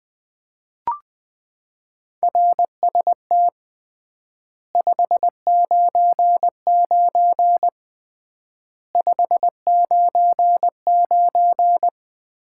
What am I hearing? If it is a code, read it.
Morse code, heard as a keyed tone, about 20 wpm, 700 Hz. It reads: RST 599 599
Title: Common QSO Phrases - 3x Word-Spacing - 20wpm